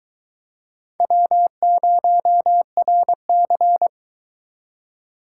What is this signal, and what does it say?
Morse code, heard as a keyed tone, about 21 wpm, 700 Hz. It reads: W0RC